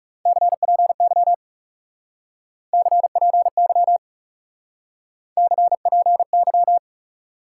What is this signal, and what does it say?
Morse code, heard as a keyed tone, about 36 wpm, 700 Hz. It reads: CPY CPY CPY
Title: Speed-Racing Top 90 QSO Elements 30wpm